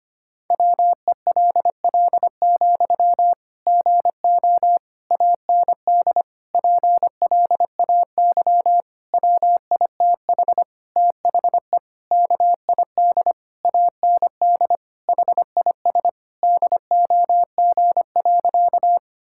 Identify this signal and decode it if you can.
Morse code, heard as a keyed tone, about 25 wpm, 700 Hz. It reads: WELL, GO AND PLAY WITH THE KID AND HIS DOG.
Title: Sentences from Top 1600 Words 25wpm